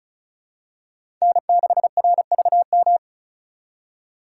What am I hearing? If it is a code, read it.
Morse code, heard as a keyed tone, about 34 wpm, 700 Hz. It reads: N6RVM
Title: USA Call Signs - 5 Characters - 2x Character Spacing - 35wpm